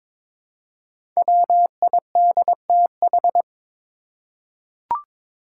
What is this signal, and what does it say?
Morse code, heard as a keyed tone, about 23 wpm, 700 Hz. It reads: WIDTH
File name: 5 Letter Words - First 3 Letters Given - 22wpm